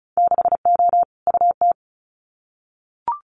Morse code, heard as a keyed tone, about 36 wpm, 700 Hz. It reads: 6O UT
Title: Winter Field Day Exchanges Encore 35wpm